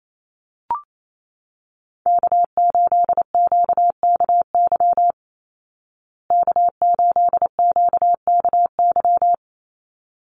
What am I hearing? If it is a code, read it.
Morse code, heard as a keyed tone, about 28 wpm, 700 Hz. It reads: K8QKY K8QKY
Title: Call Signs Repeated 3x - 28wpm